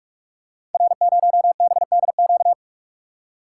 Morse code, heard as a keyed tone, about 44 wpm, 690 Hz. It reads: R0BDQ
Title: Code Groups - 5 Letter:Numbers - 45wpm